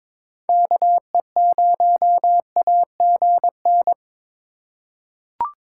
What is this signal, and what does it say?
Morse code, heard as a keyed tone, about 22 wpm, 700 Hz. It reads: KE0AGN
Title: USA Call Signs - 4x Character Spacing - 22wpm